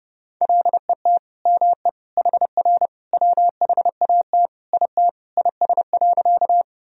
Morse code, heard as a keyed tone, about 30 wpm, 700 Hz. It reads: LET ME HR WHAT IT IS.
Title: Sentences from Top 300 Words and Abbreviations 30wpm